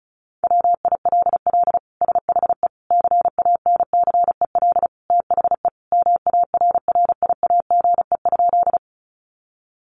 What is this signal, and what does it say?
Morse code, heard as a keyed tone, about 34 wpm, 700 Hz. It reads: WILL SHE CANCEL THE MARRIAGE?
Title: Sentences from Top 1200 Words 35wpm